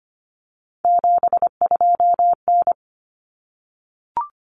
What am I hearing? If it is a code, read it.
Morse code, heard as a keyed tone, about 25 wpm, 700 Hz. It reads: 72N
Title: Common Cut Numbers - 3-Digit 25wpm